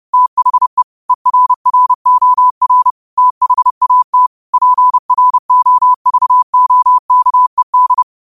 Morse code, heard as a keyed tone, about 30 wpm, 1000 Hz. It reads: THE ERROR THAT PROVOKED
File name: qotdfiles / 30wpm